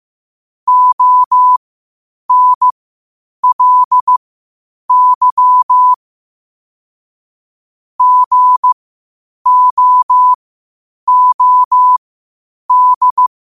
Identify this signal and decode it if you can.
Morse code, heard as a keyed tone, about 15 wpm, spaced as for 8 wpm, 1000 Hz. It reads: ONLY GOOD
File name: qotdfiles / 8wpm